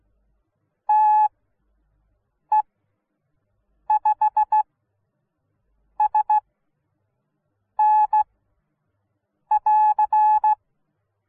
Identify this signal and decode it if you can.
Morse code, heard as a keyed tone, about 16 wpm, spaced as for 6 wpm, 850 Hz. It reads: TE5SN <AR>